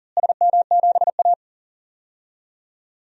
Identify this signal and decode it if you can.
Morse code, heard as a keyed tone, about 39 wpm, 700 Hz. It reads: SM7A
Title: DXCC-100 Call Signs - 4 Characters - Repeated 3x - 40wpm